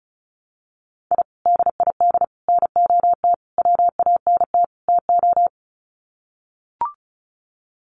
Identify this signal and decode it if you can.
Morse code, heard as a keyed tone, about 34 wpm, 700 Hz. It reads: I DID NOT WANT TO
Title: Sets of 5 Words 35wpm